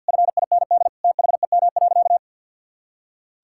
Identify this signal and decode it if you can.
Morse code, heard as a keyed tone, about 52 wpm, 700 Hz. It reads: FIND THEM.
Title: Sentences from Top 1000 Words - Review Part 6 - 50wpm